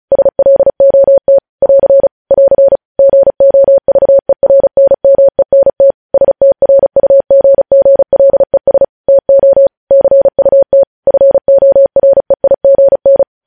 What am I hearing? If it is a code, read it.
Morse code, heard as a keyed tone, about 36 wpm, 550 Hz. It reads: SLOT <AR> <AR> GOVERNMENT STRUGGLES TO CUT FOREIGN
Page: news headlines in morse code at 35 wpm